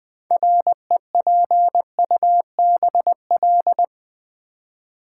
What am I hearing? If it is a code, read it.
Morse code, heard as a keyed tone, about 20 wpm, 700 Hz. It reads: REPUBL
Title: Build A Word - 2 Letters At A Time - 8-Letter Words - 20wpm